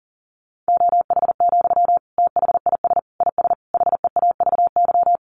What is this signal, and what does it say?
Morse code, heard as a keyed tone, about 40 wpm, 700 Hz. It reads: OH, THIS IS HEAVY
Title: Sentences from Top 1100 Words 40wpm